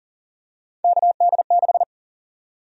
Morse code, heard as a keyed tone, about 40 wpm, 700 Hz. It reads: KD6
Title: USA Call Sign Prefixes - Repeated 2x - 40wpm